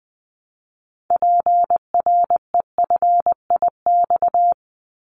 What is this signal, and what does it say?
Morse code, heard as a keyed tone, about 20 wpm, 700 Hz. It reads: PREFIX
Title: Callsign Prefixes for Top 50 DXCC - 20wpm